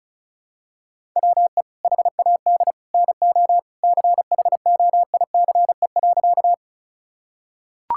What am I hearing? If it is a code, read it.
Morse code, heard as a keyed tone, about 34 wpm, 700 Hz. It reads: WE HAD NO CHOICE.
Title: Sentences from Top 1000 Words - Review Part 6 - 35wpm